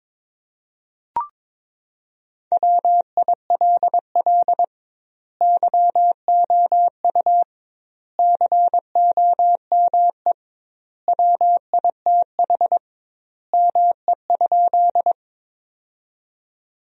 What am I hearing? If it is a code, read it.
Morse code, heard as a keyed tone, about 22 wpm, 700 Hz. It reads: WILL YOU COME WITH ME?
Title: 2x Word-Spacing - Sentences from Top 200 Words - 22wpm